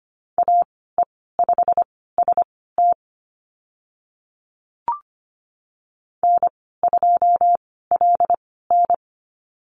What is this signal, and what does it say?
Morse code, heard as a keyed tone, about 25 wpm, spaced as for 16 wpm, 700 Hz. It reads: AE5ST N2LN
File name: Lesson 15g - Number 5 - Call Signs 25wpm